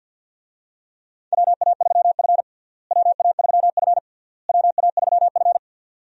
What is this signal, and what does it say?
Morse code, heard as a keyed tone, about 52 wpm, 700 Hz. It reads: WA3F WA3F WA3F
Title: USA Call Signs - 4 Characters - Repeated 3x - 50wpm